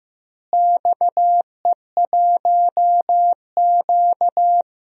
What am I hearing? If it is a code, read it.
Morse code, heard as a keyed tone, about 15 wpm, 700 Hz. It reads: XE1Q
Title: DXCC-100 Call Signs - 4 Characters - Repeated 2x - 15wpm